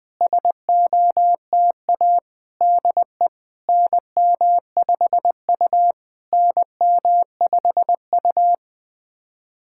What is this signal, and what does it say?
Morse code, heard as a keyed tone, about 20 wpm, 700 Hz. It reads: SOTA DE NM5U NM5U
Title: SOTA QSO Practice 20wpm